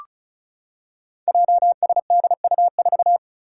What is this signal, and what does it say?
Morse code, heard as a keyed tone, about 36 wpm, 700 Hz. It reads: JSDU4
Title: Five Challenging Letter-Number Combinations 35wpm